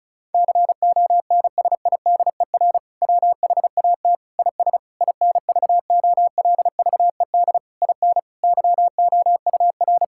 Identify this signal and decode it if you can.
Morse code, heard as a keyed tone, about 36 wpm, 700 Hz. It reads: CONSIDER WHAT IS INVOLVED IN YOUR
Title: Sentences from Top 1000 Words - Review Part 5 - 35wpm